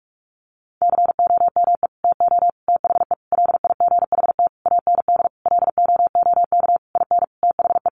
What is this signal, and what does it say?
Morse code, heard as a keyed tone, about 44 wpm, 700 Hz. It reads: COME TO THE LIGHT AND LOOK IN THE
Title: Sentences from Top 1300 Words 45wpm